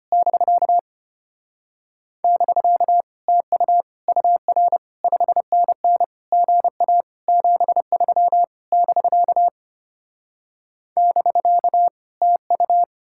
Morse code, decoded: <BK> <BK> TU UR 5NN GA 73 <BK> <BK> TU

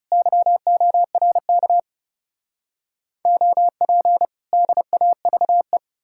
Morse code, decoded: YORK OP DAVE

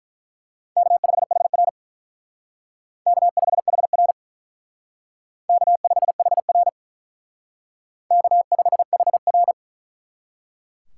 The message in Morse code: K5HR K5HR K5HR K5HR

